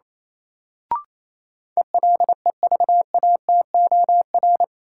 Morse code, heard as a keyed tone, about 28 wpm, 700 Hz. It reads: ELEVATOR
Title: Sentences from Top 1600 Words - New Word Drill Alt - 28wpm